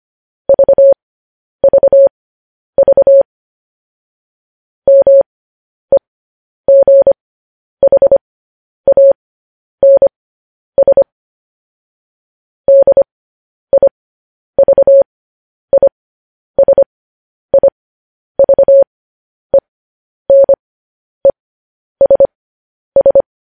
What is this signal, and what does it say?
Morse code, heard as a keyed tone, about 25 wpm, spaced as for 10 wpm, 550 Hz. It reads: VVV MEGHANS DIVISIVENESS
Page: news headlines in morse code at 10 wpm